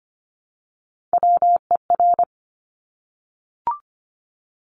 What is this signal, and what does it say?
Morse code, heard as a keyed tone, about 25 wpm, 700 Hz. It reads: WER
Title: Top Three Letter Combinations in Words 25wpm